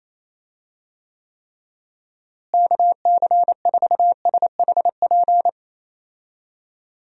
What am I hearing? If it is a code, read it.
Morse code, heard as a keyed tone, about 28 wpm, 700 Hz. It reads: KC4SHP
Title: USA Call Signs - 6 Characters - Repeated 3x - 28wpm